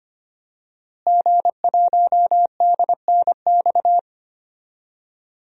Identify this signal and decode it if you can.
Morse code, heard as a keyed tone, about 25 wpm, 700 Hz. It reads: G1DNX
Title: DXCC-100 Call Signs - 5 Characters - Repeated 3x - 25wpm